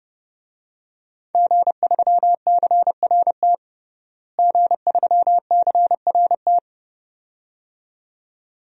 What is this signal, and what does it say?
Morse code, heard as a keyed tone, about 30 wpm, 700 Hz. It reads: G3CRT G3CRT
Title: DXCC-100 Call Signs - 5 Characters - Repeated 2x - 30wpm